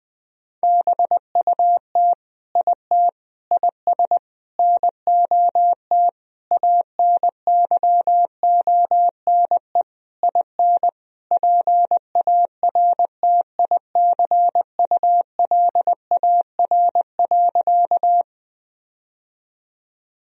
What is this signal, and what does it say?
Morse code, heard as a keyed tone, about 20 wpm, 700 Hz. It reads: BUT IT IS NOT ANYONE IN PARTICULAR.
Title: Sentences from Top 1000 Words - Review Part 4 - 20wpm